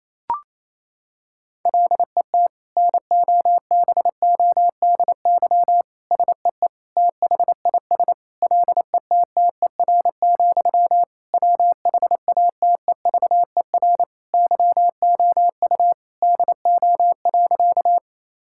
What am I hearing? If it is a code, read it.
Morse code, heard as a keyed tone, about 28 wpm, 700 Hz. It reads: LET NOBODY SEE THIS LETTER, WHATEVER YOU DO.
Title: Sentences from Top 1600 Words 28wpm